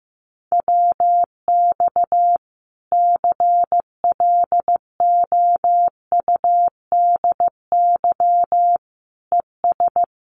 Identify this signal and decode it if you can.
Morse code, heard as a keyed tone, about 15 wpm, 700 Hz. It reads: WX CLOUDY ES
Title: Common QSO Phrases 15wpm